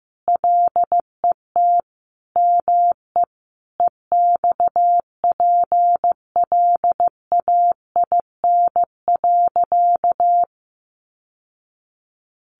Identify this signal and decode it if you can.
Morse code, heard as a keyed tone, about 15 wpm, 700 Hz. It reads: LET ME EXPLAIN.
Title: Sentences from Top 1100 Words 15wpm